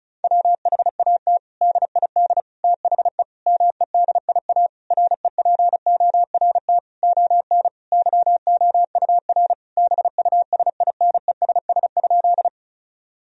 Morse code, heard as a keyed tone, about 36 wpm, 700 Hz. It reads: WHAT DID THE MEDIA REPORT ON YOUR BUSINESS?
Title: Sentences from Top 1000 Words - Review Part 4 - 35wpm